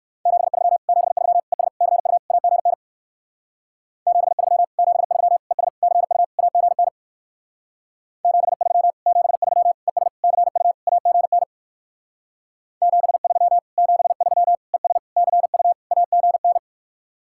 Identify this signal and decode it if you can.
Morse code, heard as a keyed tone, about 58 wpm, 700 Hz. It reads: 73 73 ES CU AGN 73 73 ES CU AGN 73 73 ES CU AGN 73 73 ES CU AGN